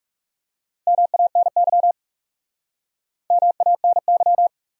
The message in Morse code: MANY MANY

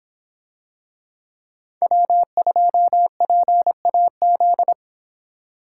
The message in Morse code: W2PAZ

